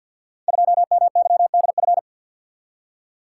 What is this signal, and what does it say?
Morse code, heard as a keyed tone, about 51 wpm, 700 Hz. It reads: 2MYDF
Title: Code Groups - 5 Letter:Numbers - 50wpm